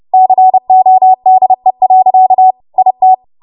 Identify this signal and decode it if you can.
Morse code, heard as a keyed tone, about 30 wpm, 750 Hz. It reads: CODE. IT
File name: Morse / 30WPM